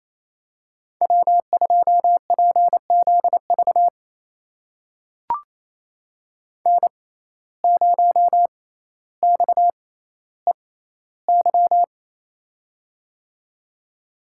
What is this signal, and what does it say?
Morse code, heard as a keyed tone, about 28 wpm, spaced as for 14 wpm, 700 Hz. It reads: W2PZV N0XEY
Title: USA Call Signs - 5 Characters - 6x Character Spacing - 28wpm